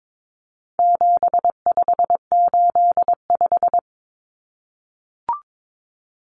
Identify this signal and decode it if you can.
Morse code, heard as a keyed tone, about 22 wpm, 700 Hz. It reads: 7585
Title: Common Cut Numbers - 4-Digit 22wpm